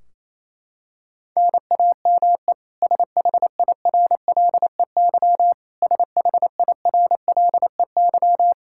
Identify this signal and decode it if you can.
Morse code, heard as a keyed tone, about 28 wpm, 700 Hz. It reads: NAME SHIRLEY SHIRLEY
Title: K1USN SST QSO Practice 28wpm